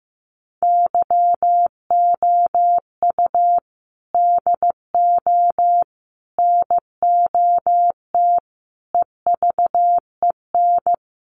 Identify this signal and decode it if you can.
Morse code, heard as a keyed tone, about 14 wpm, 700 Hz. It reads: YOU DO NOT EVEN
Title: Sets of 5 Words Encore2 15wpm